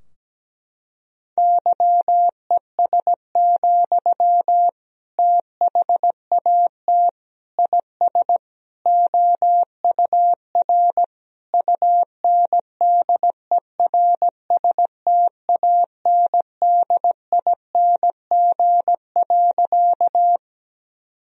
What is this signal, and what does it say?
Morse code, heard as a keyed tone, about 17 wpm, 700 Hz. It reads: YES, THAT IS OUR UNDERSTANDING.
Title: Sentences from Top 1000 Words - Review Part 3 - 17wpm